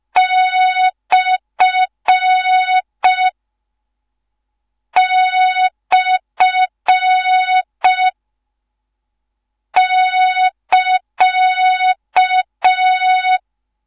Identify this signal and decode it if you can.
Morse code, heard as a keyed tone, about 5 wpm, 3700 Hz. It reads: / / <KA>
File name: Morze-HA2MN / hang / 13IRASJELEK1 HA2MN